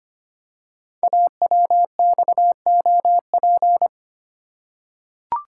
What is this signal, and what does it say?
Morse code, heard as a keyed tone, about 25 wpm, 690 Hz. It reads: AWXOP
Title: Code Groups - 5 Letters - 25wpm